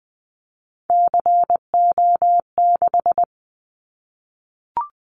Morse code, heard as a keyed tone, about 20 wpm, 700 Hz. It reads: CO6
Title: Callsign Prefixes for Top 100 DXCC - 20wpm